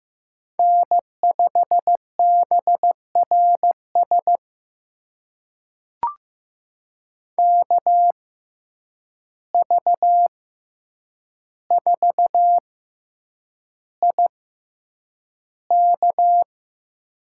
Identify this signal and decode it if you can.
Morse code, heard as a keyed tone, about 15 wpm, spaced as for 8 wpm, 700 Hz. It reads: N5BRS KV4IK
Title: USA Call Signs - 5 Characters - 6x Character Spacing - 15wpm